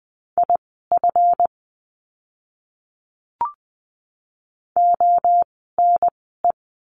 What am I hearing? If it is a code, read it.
Morse code, heard as a keyed tone, about 20 wpm, spaced as for 14 wpm, 700 Hz. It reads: IF ONE
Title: 139 Lesson 23f - Letter f - Words 20wpm